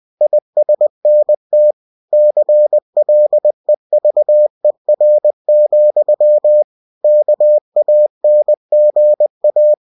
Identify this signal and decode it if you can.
Morse code, heard as a keyed tone, about 20 wpm, 600 Hz. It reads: ISNT CLEVER, KANGA
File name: cw-books / winnie-the-pooh / winnie-20wpm-page-0092